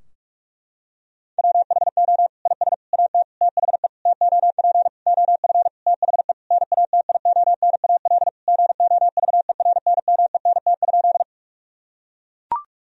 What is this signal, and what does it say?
Morse code, heard as a keyed tone, about 43 wpm, 700 Hz. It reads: WHO IS AT THE TOP OF THE NATIONAL GOVERNMENT?
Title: Sentences from Top 1000 Words - Review Part 4 - 45wpm